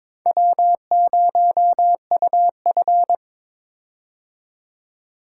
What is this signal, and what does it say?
Morse code, heard as a keyed tone, about 22 wpm, 700 Hz. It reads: W0UF
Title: K1USN SST QSO Practice - 22wpm